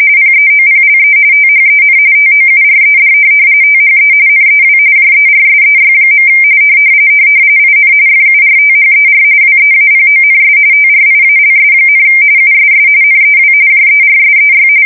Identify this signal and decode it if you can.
RTTY, about 45.45 baud. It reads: PEOPLE THEN TOURED THE NEW JERSEY. IN FURTHER WEAPONS TRIALS, USS NEW JERSEY
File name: fsk170 / USS-New-Jersey